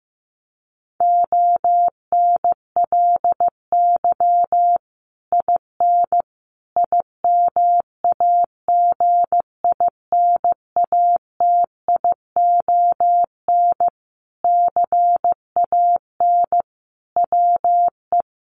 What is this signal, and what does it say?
Morse code, heard as a keyed tone, about 15 wpm, 700 Hz. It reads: ONLY IN IMAGINATION CAN WE
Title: Sentences from Top 1300 Words 15wpm